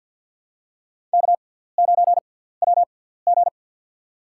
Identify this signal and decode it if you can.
Morse code, heard as a keyed tone, about 51 wpm, spaced as for 17 wpm, 700 Hz. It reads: K9WG
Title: USA Call Signs - 4 Characters - 6x Character Spacing - 50wpm